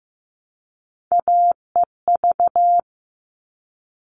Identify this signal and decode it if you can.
Morse code, heard as a keyed tone, about 15 wpm, 700 Hz. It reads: AEV
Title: All Cut Numbers - 3-Digit 15wpm